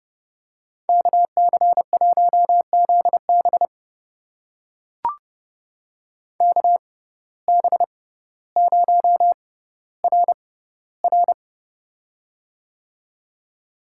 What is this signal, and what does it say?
Morse code, heard as a keyed tone, about 30 wpm, spaced as for 15 wpm, 700 Hz. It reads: KC1ZB KB0RR
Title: USA Call Signs - 5 Characters - 6x Character Spacing - 30wpm